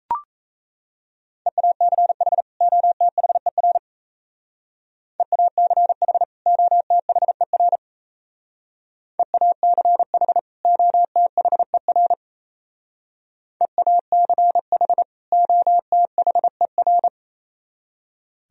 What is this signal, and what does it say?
Morse code, heard as a keyed tone, about 34 wpm, 690 Hz. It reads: EACH OTHER EACH OTHER EACH OTHER EACH OTHER
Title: Speed-Racing - Sets of 2 Words 28wpm